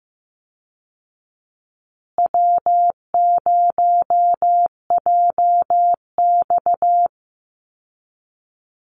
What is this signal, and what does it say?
Morse code, heard as a keyed tone, about 15 wpm, 700 Hz. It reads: W0JX
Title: Easy Call Signs Repeated 3x - 15wpm